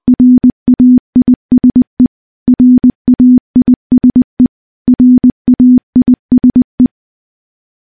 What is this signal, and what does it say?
Morse code, heard as a keyed tone, about 20 wpm, 250 Hz. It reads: RAISE RAISE RAISE
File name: morse / morse-training-20WPM